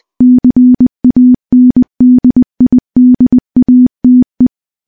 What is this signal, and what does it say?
Morse code, heard as a keyed tone, about 20 wpm, 250 Hz. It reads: CANDIDATE
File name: morse-training-20WPM